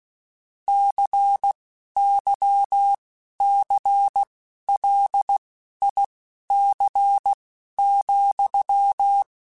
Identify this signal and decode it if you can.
Morse code, heard as a keyed tone, about 16 wpm, spaced as for 11 wpm, 800 Hz. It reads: CYCLIC,
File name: Technical Subjects / Technical Subjects-mp3s / XC @10